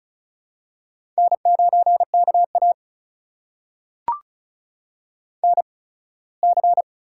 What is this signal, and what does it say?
Morse code, heard as a keyed tone, about 35 wpm, spaced as for 21 wpm, 700 Hz. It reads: N9KA NC
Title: USA Call Signs - 4 Characters - 8x Character Spacing - 35wpm